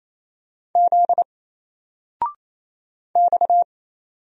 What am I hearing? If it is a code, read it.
Morse code, heard as a keyed tone, about 27 wpm, 700 Hz. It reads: Z X
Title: Single Letter-Number 28wpm